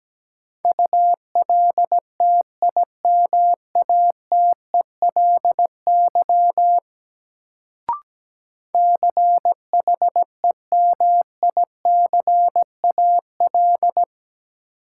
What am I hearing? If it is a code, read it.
Morse code, heard as a keyed tone, about 17 wpm, 700 Hz. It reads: ULTIMATELY CHEMICAL
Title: Sentences from Top 1200 Words - New Word Drill - 17wpm